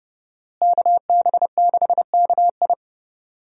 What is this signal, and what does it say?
Morse code, heard as a keyed tone, about 32 wpm, 700 Hz. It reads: KB6KI
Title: USA Call Signs - 5 Characters - Repeated 3x - 30wpm